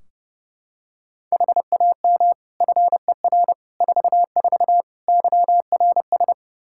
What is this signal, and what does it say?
Morse code, heard as a keyed tone, about 30 wpm, 700 Hz. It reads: HAM FER 44 YRS
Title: K1USN SST QSO Practice 30wpm